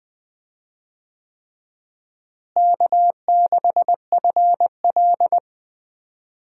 Morse code, heard as a keyed tone, about 21 wpm, 700 Hz. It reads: K6FL
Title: USA Call Signs - 4 Characters - Repeated 3x - 20wpm